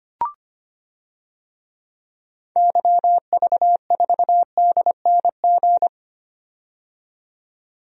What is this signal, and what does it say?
Morse code, heard as a keyed tone, about 25 wpm, 700 Hz. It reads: YV4DNG
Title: DXCC-100 Call Signs - 6 Characters - 25wpm